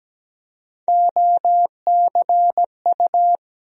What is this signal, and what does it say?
Morse code, heard as a keyed tone, about 17 wpm, 700 Hz. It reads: OCU
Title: USA Call Sign Postfixes - 3 Letters 17wpm